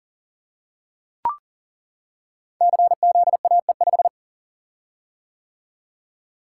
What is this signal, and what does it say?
Morse code, heard as a keyed tone, about 42 wpm, 700 Hz. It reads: CZAE5
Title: Code Groups - 5 Letter:Numbers - 40wpm